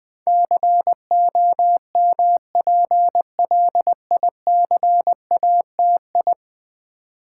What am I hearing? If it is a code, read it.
Morse code, heard as a keyed tone, about 20 wpm, 700 Hz. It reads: COMPLICATI
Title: Build A Word - 5 Letters At A Time - 12-Letter Words - 20wpm